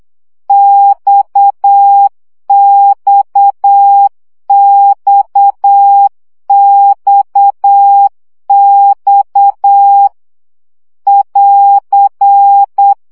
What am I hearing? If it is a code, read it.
Morse code, heard as a keyed tone, about 8 wpm, 800 Hz. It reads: XXXXX <AR>
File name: CW gyakorlas x 7 baud